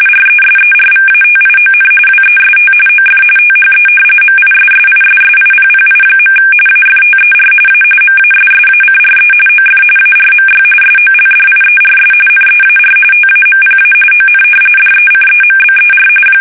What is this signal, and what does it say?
RTTY, about 45.45 baud. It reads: RESERVATION WEST OF THE MISSISSIPPI. AFTER A MASSACRE OF ARMY SOLDIERS NEAR TAMPA ON DECEMBER
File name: fsk850 / usnavy2